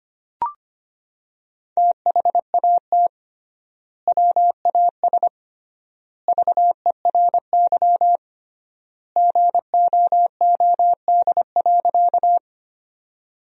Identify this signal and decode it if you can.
Morse code, heard as a keyed tone, about 25 wpm, 700 Hz. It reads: THAT WAS VERY GOOD.
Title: 3x Word-Spacing - Sentences from Top 200 Words - 25wpm